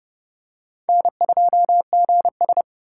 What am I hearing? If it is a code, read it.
Morse code, heard as a keyed tone, about 30 wpm, 700 Hz. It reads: N2GS
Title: USA Call Signs - 4 Characters - 4x Character Spacing - 30wpm